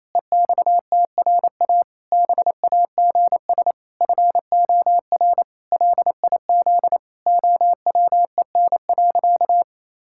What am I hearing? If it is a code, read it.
Morse code, heard as a keyed tone, about 28 wpm, 700 Hz. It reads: EXTRA BAGS FOR LIZ OWEN.
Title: Pangram Challenge v2 - 28wpm